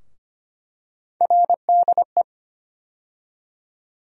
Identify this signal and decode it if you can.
Morse code, heard as a keyed tone, about 25 wpm, 700 Hz. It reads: RDE
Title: USA Call Sign Postfixes - 3 Letters 25wpm